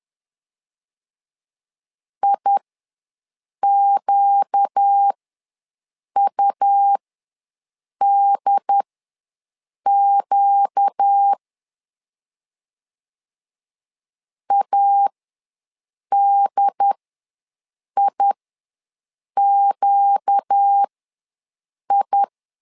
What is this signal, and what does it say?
Morse code, heard as a keyed tone, about 10 wpm, spaced as for 6 wpm, 800 Hz. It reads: IQUDQ ADIQI
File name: LEZIONI / 4 Q ws 10wpm 2306